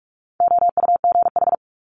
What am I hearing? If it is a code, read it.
Morse code, heard as a keyed tone, about 46 wpm, 700 Hz. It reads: OUGH
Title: Multi-letter Phonemes Quick Fire 45wpm